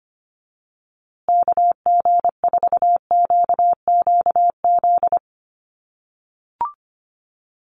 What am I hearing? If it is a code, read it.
Morse code, heard as a keyed tone, about 25 wpm, 700 Hz. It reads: KG4QQZ
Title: USA Call Signs Repeated 2x 25wpm